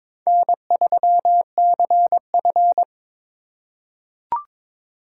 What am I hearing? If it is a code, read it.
Morse code, heard as a keyed tone, about 22 wpm, 700 Hz. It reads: N3CF
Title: USA Call Signs - 4 Characters - 2x Character Spacing - 22wpm